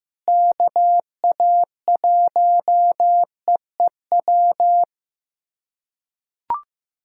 Morse code, heard as a keyed tone, about 15 wpm, 700 Hz. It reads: KA1EEW
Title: USA Callsign DE Callsign 15wpm